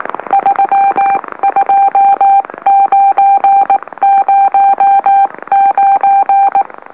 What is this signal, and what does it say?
Morse code, heard as a keyed tone, about 18 wpm, 790 Hz. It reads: 32909
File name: M04- -2345utc-LOLO